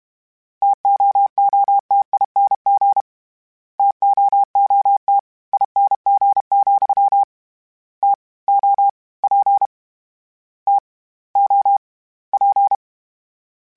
Morse code, heard as a keyed tone, about 32 wpm, 800 Hz. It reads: TOOTING TOOT ING, T O P T O P
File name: Variable Spacing / Variable Spacing-mp3s / TOs @30